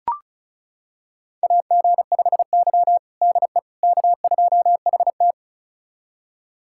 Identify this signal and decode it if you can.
Morse code, heard as a keyed tone, about 36 wpm, 700 Hz. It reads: AG5Y DE K2HT
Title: USA Callsign DE Callsign - 4 Characters - 35wpm